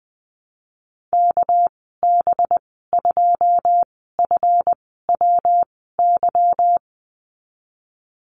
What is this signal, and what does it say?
Morse code, heard as a keyed tone, about 19 wpm, spaced as for 14 wpm, 700 Hz. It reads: KB2FWY
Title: 200 Lesson 32g - Letter k - Call Signs 20wpm